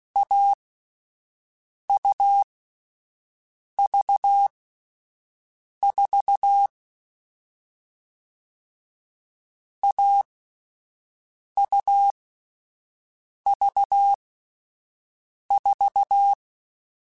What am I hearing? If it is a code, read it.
Morse code, heard as a keyed tone, about 16 wpm, spaced as for 5 wpm, 800 Hz. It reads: AUV4 AUV4
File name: Difficult Combos / Difficult Combos-mp3s / AUV AUV4 @5